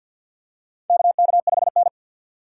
Morse code, heard as a keyed tone, about 52 wpm, 700 Hz. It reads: KK5N
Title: USA Call Signs - 4 Characters - 2x Character Spacing - 50wpm